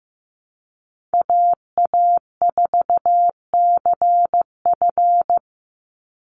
Morse code, heard as a keyed tone, about 15 wpm, 700 Hz. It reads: AA4CF
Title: Call Signs Repeated 2x - 15wpm